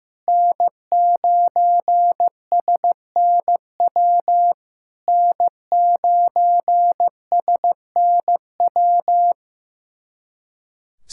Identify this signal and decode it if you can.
Morse code, heard as a keyed tone, about 15 wpm, 700 Hz. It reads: N9SNW N9SNW
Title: SOTA QSO Practice 15wpm